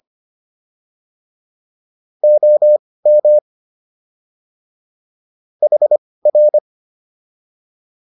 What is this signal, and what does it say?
Morse code, heard as a keyed tone, about 25 wpm, spaced as for 18 wpm, 600 Hz. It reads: OM HR